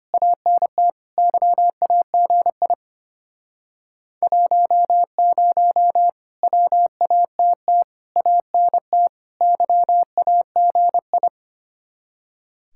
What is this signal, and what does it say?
Morse code, heard as a keyed tone, about 33 wpm, 700 Hz. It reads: ANT YAGI 10 WATT ANT YAGI